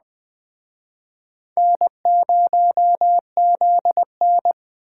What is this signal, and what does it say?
Morse code, heard as a keyed tone, about 24 wpm, 700 Hz. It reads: N0ZN
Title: Speed Racing - USA 1x2 Call Signs 20wpm